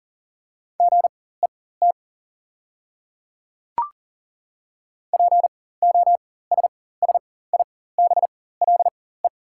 Morse code, heard as a keyed tone, about 39 wpm, spaced as for 19 wpm, 690 Hz. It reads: GET POSSIBLE
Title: 4x Character Spacing - Top 500 Words - 40wpm